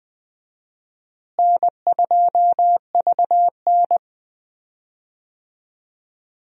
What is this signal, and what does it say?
Morse code, heard as a keyed tone, about 21 wpm, 700 Hz. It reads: N2VN
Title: USA Call Signs - 4 Characters - Repeated 2x - 20wpm